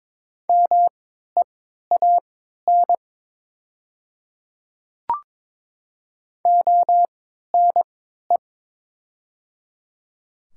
Morse code, heard as a keyed tone, about 22 wpm, spaced as for 13 wpm, 700 Hz. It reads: MEAN ONE
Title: 3x Character Spacing - Top 500 Words - 22wpm